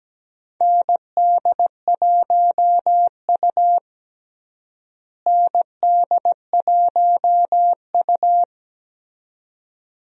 Morse code, heard as a keyed tone, about 17 wpm, 700 Hz. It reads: ND1U ND1U
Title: USA Call Signs - 4 Characters - Repeated 2x - 17wpm